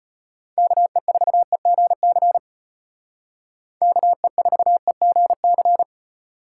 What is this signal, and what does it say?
Morse code, heard as a keyed tone, about 32 wpm, 700 Hz. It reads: KE4EGC KE4EGC